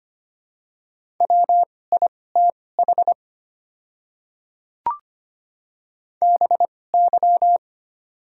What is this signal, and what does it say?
Morse code, heard as a keyed tone, about 25 wpm, spaced as for 18 wpm, 700 Hz. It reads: WITH BY
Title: 2x Character Spacing - Top 100 Words - 25wpm